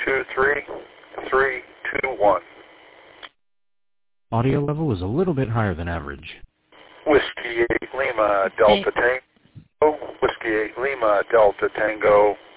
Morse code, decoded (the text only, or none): none